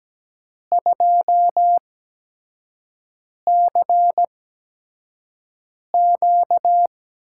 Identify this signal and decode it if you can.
Morse code, heard as a keyed tone, about 17 wpm, spaced as for 5 wpm, 700 Hz. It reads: 2CQ
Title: USA Call Signs - 5 Characters - 8x Character Spacing - 17wpm